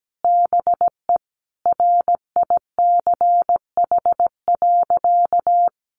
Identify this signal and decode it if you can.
Morse code, heard as a keyed tone, about 17 wpm, 700 Hz. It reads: BE RICH.
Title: Sentences from Top 1100 Words 17wpm